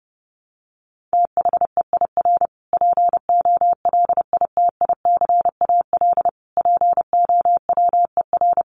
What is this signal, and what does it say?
Morse code, heard as a keyed tone, about 30 wpm, 700 Hz. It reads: THEIR POLITICAL POWER
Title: Sentences from Top 1300 Words 30wpm